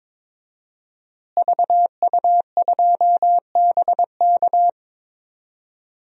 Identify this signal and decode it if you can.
Morse code, heard as a keyed tone, about 22 wpm, 700 Hz. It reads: VU2BK
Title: DXCC-100 Call Signs - 5 Characters - Repeated 3x - 22wpm